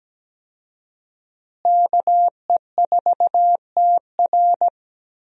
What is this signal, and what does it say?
Morse code, heard as a keyed tone, about 17 wpm, 700 Hz. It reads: KE4TR
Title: USA Call Signs - 5 Characters - Repeated 3x - 17wpm